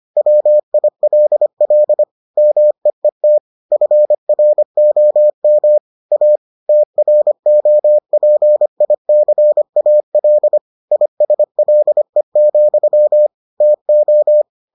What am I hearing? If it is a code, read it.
Morse code, heard as a keyed tone, about 25 wpm, 600 Hz. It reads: WILL MEET FROM A TROPICAL ISLE, TO